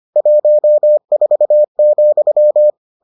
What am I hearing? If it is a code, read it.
Morse code, heard as a keyed tone, about 25 wpm, 600 Hz. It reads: 14,